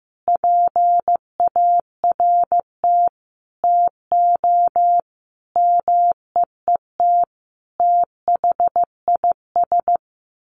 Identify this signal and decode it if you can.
Morse code, heard as a keyed tone, about 15 wpm, 700 Hz. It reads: PART TO MEET THIS
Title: Sentences from Top 1100 Words 15wpm